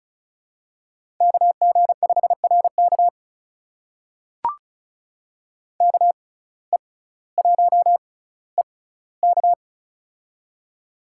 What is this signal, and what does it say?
Morse code, heard as a keyed tone, about 36 wpm, spaced as for 18 wpm, 700 Hz. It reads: KG5RK KE1EK
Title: USA Call Signs - 5 Characters - 6x Character Spacing - 35wpm